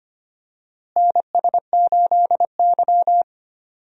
Morse code, heard as a keyed tone, about 25 wpm, 700 Hz. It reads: NS8Y
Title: USA Call Signs - 4 Characters - Repeated 2x - 25wpm